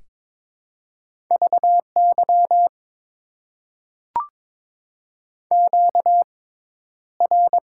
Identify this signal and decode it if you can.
Morse code, heard as a keyed tone, about 22 wpm, spaced as for 11 wpm, 700 Hz. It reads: VY QR
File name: Top 90 QSO Elements - 6x Character Spacing - 22wpm